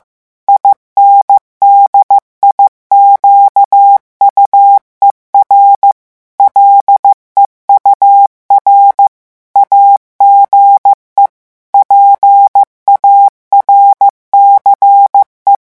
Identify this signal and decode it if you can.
Morse code, heard as a keyed tone, about 15 wpm, 800 Hz. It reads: INDIQUER LEUR AGE PARCE
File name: morse mp3 / revision9 citations humoristiques vitesse 14 mots minute